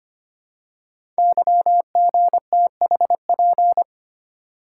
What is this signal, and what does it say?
Morse code, heard as a keyed tone, about 25 wpm, 700 Hz. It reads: YGTHP
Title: Code Groups - 5 Letters - 25wpm